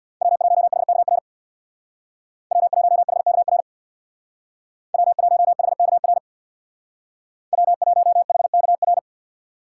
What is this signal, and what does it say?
Morse code, heard as a keyed tone, about 64 wpm, 700 Hz. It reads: W1HKR W1HKR W1HKR W1HKR